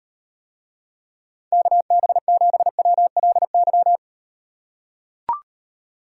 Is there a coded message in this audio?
Yes: Morse code from 1.5 to 4.0 s, heard as a keyed tone, about 38 wpm, 700 Hz. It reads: KB7WLY